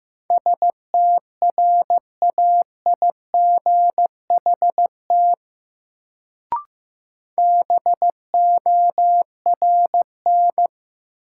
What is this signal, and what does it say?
Morse code, heard as a keyed tone, about 15 wpm, 700 Hz. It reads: STRAIGHT BORN
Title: Sentences from Top 1000 Words - New Word Drill - 15wpm